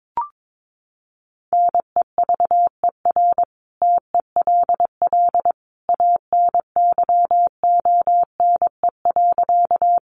Morse code, heard as a keyed tone, about 22 wpm, 700 Hz. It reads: NEVER TELL ANYONE.
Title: Sentences from Top 900 Words 22wpm